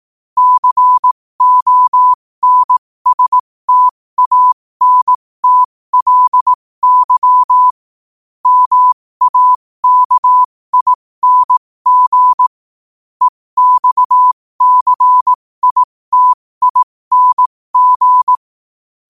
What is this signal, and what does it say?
Morse code, heard as a keyed tone, about 18 wpm, spaced as for 15 wpm, 1000 Hz. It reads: CONSTANTLY MAKING EXCITING
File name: qotdfiles / 15wpm